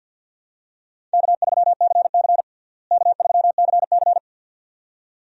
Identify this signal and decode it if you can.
Morse code, heard as a keyed tone, about 52 wpm, 700 Hz. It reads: K3CC K3CC
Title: USA Call Signs - 4 Characters - Repeated 2x - 50wpm